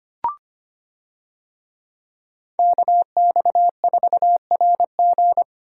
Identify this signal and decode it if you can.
Morse code, heard as a keyed tone, about 25 wpm, 700 Hz. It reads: KX4RG
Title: USA Call Signs - 5 Characters - Repeated 2x - 25wpm